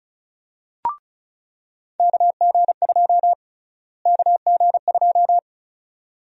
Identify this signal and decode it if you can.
Morse code, heard as a keyed tone, about 36 wpm, 700 Hz. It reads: KG2 KG2
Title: USA Call Sign Prefixes - Repeated 2x - 35wpm